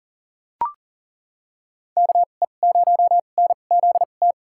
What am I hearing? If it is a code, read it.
Morse code, heard as a keyed tone, about 40 wpm, spaced as for 29 wpm, 700 Hz. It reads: KE0NZT
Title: USA Call Signs - 6 Characters - 2x Character Spacing - 40wpm